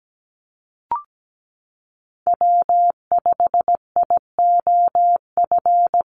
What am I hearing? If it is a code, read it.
Morse code, heard as a keyed tone, about 17 wpm, 700 Hz. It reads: W5IOF
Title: USA Call Signs 17wpm